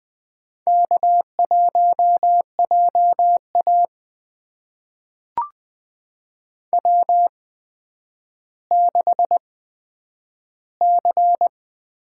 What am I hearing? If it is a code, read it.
Morse code, heard as a keyed tone, about 20 wpm, spaced as for 10 wpm, 700 Hz. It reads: K1JA W6C
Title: USA Call Signs - 4 Characters - 8x Character Spacing - 20wpm